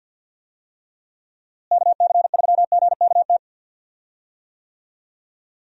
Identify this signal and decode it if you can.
Morse code, heard as a keyed tone, about 52 wpm, 700 Hz. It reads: KC3GKT
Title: USA Call Signs - 6 Characters - Repeated 3x - 50wpm